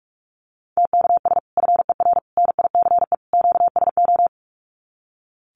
Morse code, heard as a keyed tone, about 44 wpm, 700 Hz. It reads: TKS FER NICE QSO
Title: Common QSO Phrases 45wpm